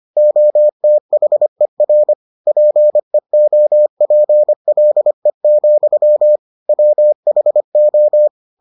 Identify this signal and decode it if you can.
Morse code, heard as a keyed tone, about 25 wpm, 600 Hz. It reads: OTHER PEOPLE, WHO